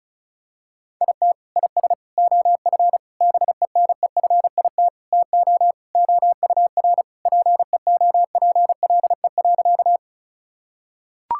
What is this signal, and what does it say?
Morse code, heard as a keyed tone, about 36 wpm, 700 Hz. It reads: IT IS OF BENEFIT TO OUR PEOPLE.
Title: Sentences from Top 1000 Words - Review Part 6 - 35wpm